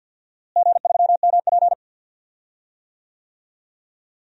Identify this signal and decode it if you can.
Morse code, heard as a keyed tone, about 52 wpm, 700 Hz. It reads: G3MP
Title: DXCC-100 Call Signs - 4 Characters - Repeated 2x - 50wpm